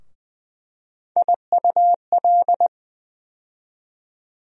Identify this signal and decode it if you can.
Morse code, heard as a keyed tone, about 20 wpm, 700 Hz. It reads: IUL
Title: USA Call Sign Postfixes - 3 Letters 20wpm